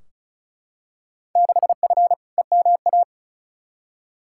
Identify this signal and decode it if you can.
Morse code, heard as a keyed tone, about 36 wpm, 690 Hz. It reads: 6F EMA
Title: Field Day Exchanges Encore - 35wpm